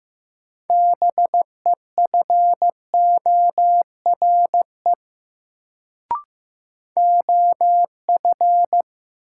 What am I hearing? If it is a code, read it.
Morse code, heard as a keyed tone, about 15 wpm, 700 Hz. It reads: BEFORE OF
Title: Sets of 2 Words - New Word Drill - 15wpm